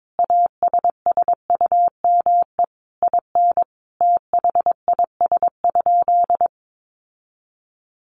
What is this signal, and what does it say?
Morse code, heard as a keyed tone, about 22 wpm, 700 Hz. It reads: ASSUME IN THIS?
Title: Sentences from Top 1500 Words 22wpm